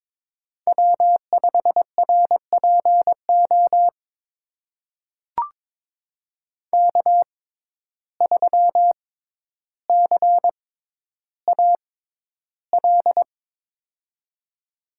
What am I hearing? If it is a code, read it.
Morse code, heard as a keyed tone, about 22 wpm, spaced as for 11 wpm, 700 Hz. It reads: W5RPO K3CAL
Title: USA Call Signs - 5 Characters - 6x Character Spacing - 22wpm